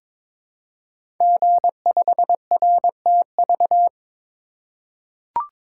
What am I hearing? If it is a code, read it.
Morse code, heard as a keyed tone, about 22 wpm, 700 Hz. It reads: G5RTV
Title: Code Groups - 5 Letter:Numbers - 22wpm